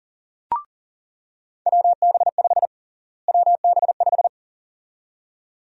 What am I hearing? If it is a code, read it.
Morse code, heard as a keyed tone, about 40 wpm, 700 Hz. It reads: WB5 WB5
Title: USA Call Sign Prefixes - Repeated 2x - 40wpm